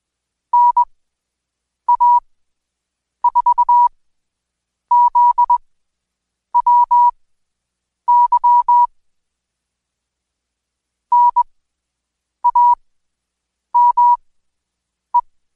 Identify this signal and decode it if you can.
Morse code, heard as a keyed tone, about 21 wpm, spaced as for 7 wpm, 970 Hz. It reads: NA4ZWY NAME
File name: Code Quick / AudioDownld2 / Seg113-Day18-DayDrill 18